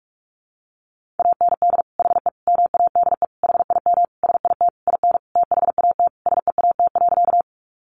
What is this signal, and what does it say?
Morse code, heard as a keyed tone, about 44 wpm, 700 Hz. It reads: AND HE MADE HIM SIT IN THAT SEAT.
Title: Sentences from Top 1500 Words 45wpm